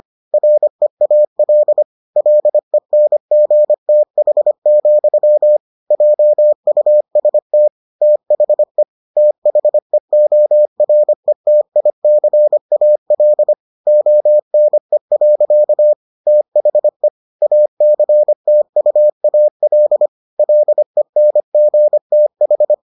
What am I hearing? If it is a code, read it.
Morse code, heard as a keyed tone, about 25 wpm, 600 Hz. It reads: REAL LENGTH, JUST THE THEORETICAL ONE. THE ACTUAL LENGTH